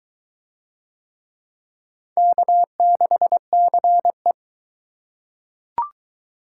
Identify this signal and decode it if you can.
Morse code, heard as a keyed tone, about 22 wpm, 700 Hz. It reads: K6CE